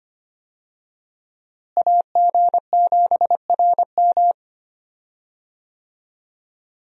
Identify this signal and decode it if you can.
Morse code, heard as a keyed tone, about 25 wpm, 700 Hz. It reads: AG7RM
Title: USA Call Signs - 5 Characters - Repeated 3x - 25wpm